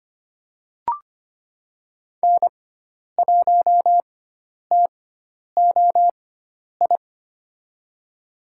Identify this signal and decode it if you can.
Morse code, heard as a keyed tone, about 25 wpm, spaced as for 10 wpm, 700 Hz. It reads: N1TOI
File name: Lesson 16g - Letter c - Call Signs 25wpm